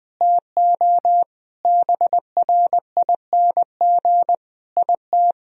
Morse code, heard as a keyed tone, about 20 wpm, 700 Hz. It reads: TO BRING IT
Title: Sentences from Top 1000 Words - Review Part 4 - 20wpm